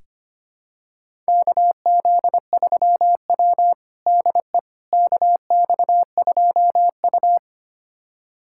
Morse code, heard as a keyed tone, about 25 wpm, 700 Hz. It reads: KZ3W DE KX2U
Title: USA Callsign DE Callsign - 4 Characters - 25wpm